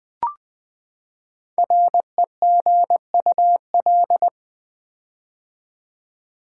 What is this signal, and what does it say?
Morse code, heard as a keyed tone, about 20 wpm, 700 Hz. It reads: REGUL
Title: Build A Word - 5 Letters At A Time - 10-Letter Words - 20wpm